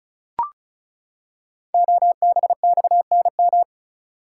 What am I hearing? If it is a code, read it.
Morse code, heard as a keyed tone, about 36 wpm, 700 Hz. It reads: OBXNM
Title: Code Groups - 5 Letters - 35wpm